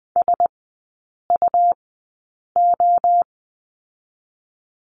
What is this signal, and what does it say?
Morse code, heard as a keyed tone, about 20 wpm, spaced as for 8 wpm, 700 Hz. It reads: SUO
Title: 145 Lesson 24e - Letter y - Three Characters 20wpm